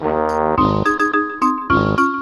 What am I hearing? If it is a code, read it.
There is no code.